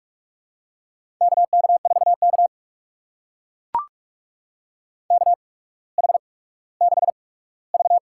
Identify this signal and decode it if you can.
Morse code, heard as a keyed tone, about 44 wpm, spaced as for 19 wpm, 700 Hz. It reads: KK4K KH6V
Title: USA Call Signs - 4 Characters - 8x Character Spacing - 45wpm